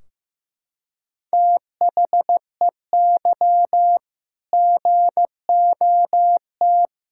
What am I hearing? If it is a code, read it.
Morse code, heard as a keyed tone, about 15 wpm, 700 Hz. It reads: THEY GOT
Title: Sentences from Top 1600 Words 15wpm